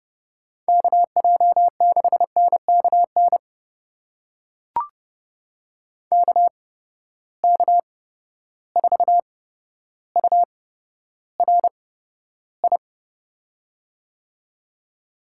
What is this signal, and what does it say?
Morse code, heard as a keyed tone, about 31 wpm, spaced as for 13 wpm, 700 Hz. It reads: KJ6NKN KK4URI
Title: USA Call Signs - 6 Characters - 8x Character Spacing - 30wpm